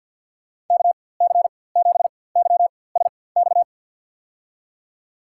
Morse code, heard as a keyed tone, about 45 wpm, spaced as for 23 wpm, 690 Hz. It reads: KC7YSX
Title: USA Call Signs - 4x Character Spacing - 50wpm